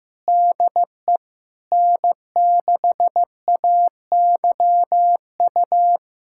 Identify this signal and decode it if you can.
Morse code, heard as a keyed tone, about 15 wpm, 700 Hz. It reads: DE N6AYU
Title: USA Callsign DE Callsign - 5 Characters - 15wpm